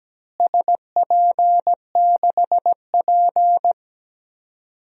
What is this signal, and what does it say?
Morse code, heard as a keyed tone, about 17 wpm, 700 Hz. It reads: SP6P
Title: DXCC-100 Call Signs - 4 Characters - Repeated 2x - 17wpm